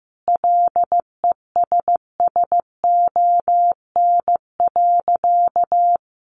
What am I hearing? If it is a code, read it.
Morse code, heard as a keyed tone, about 15 wpm, 700 Hz. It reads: LESSON.
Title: Sentences from Top 1200 Words 15wpm